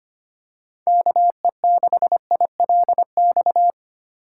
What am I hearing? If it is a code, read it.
Morse code, heard as a keyed tone, about 25 wpm, 700 Hz. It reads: KE6ILX
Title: USA Call Signs - 6 Characters - 3x Character Spacing - 25wpm